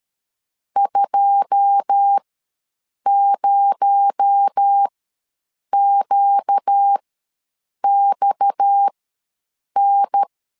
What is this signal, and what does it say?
Morse code, heard as a keyed tone, about 13 wpm, spaced as for 7 wpm, 800 Hz. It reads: 20QXN
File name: LEZIONI / 10 SIGNS / 10B INT ws 12wpm 3527